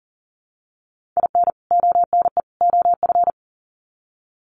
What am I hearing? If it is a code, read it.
Morse code, heard as a keyed tone, about 40 wpm, 700 Hz. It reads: IN ONE OF
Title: Sets of 3 Words 40wpm